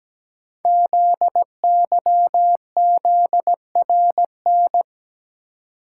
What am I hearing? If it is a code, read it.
Morse code, heard as a keyed tone, about 17 wpm, 700 Hz. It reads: ZYZRN
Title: Code Groups - 5 Letters - 17wpm